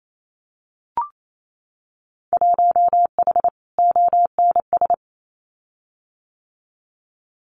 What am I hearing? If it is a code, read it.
Morse code, heard as a keyed tone, about 28 wpm, 700 Hz. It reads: 1H ONS
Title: Winter Field Day Exchanges Encore 28wpm